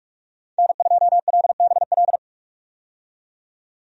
Morse code, heard as a keyed tone, about 46 wpm, 700 Hz. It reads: N2LBL
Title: USA Call Signs - 5 Characters - 45wpm